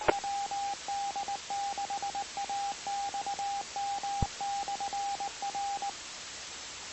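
Morse code, heard as a keyed tone, about 19 wpm, 800 Hz, with radio noise. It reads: WD6AXM/R
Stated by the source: Morse recording on the 2 m band, 146.085 MHz.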